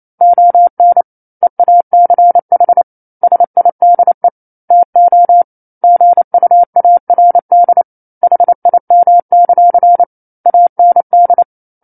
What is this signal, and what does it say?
Morse code, heard as a keyed tone, about 28 wpm, 700 Hz. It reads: ON EACH SIDE TO GUARD HIM; AND
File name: 09-Chapter-09 28 WPM